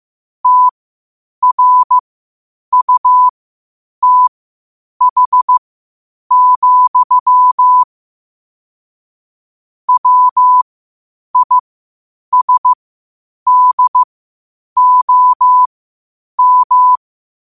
Morse code, heard as a keyed tone, about 15 wpm, spaced as for 8 wpm, 1000 Hz. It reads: TRUTH, WISDOM
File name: qotdfiles / 8wpm